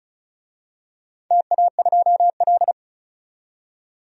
Morse code, heard as a keyed tone, about 34 wpm, 700 Hz. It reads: TA2L
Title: DXCC-100 Call Signs - 4 Characters - 35wpm